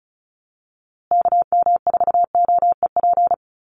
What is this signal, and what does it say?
Morse code, heard as a keyed tone, about 34 wpm, 700 Hz. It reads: KM4OEP
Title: USA Call Signs Repeated 3x 35wpm